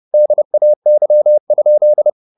Morse code, heard as a keyed tone, about 30 wpm, 600 Hz. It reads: DAY?
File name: the old man and the sea / the old man and the sea 1 30wpm